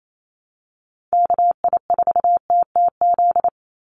Morse code, heard as a keyed tone, about 28 wpm, 700 Hz. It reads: KI4TTZ
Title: USA Call Signs 28wpm